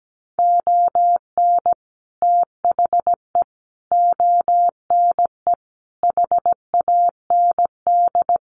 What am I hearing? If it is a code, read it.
Morse code, heard as a keyed tone, about 17 wpm, 700 Hz. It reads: ON THE ONE HAND
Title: Sets of 4 Words 17wpm